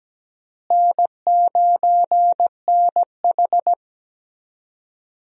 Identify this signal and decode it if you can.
Morse code, heard as a keyed tone, about 17 wpm, 700 Hz. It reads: N9NH
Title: USA Call Signs - 4 Characters - Repeated 3x - 17wpm